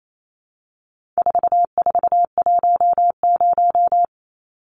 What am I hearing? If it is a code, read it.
Morse code, heard as a keyed tone, about 28 wpm, 700 Hz. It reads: 4410
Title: Numbers - 4-Digit 28wpm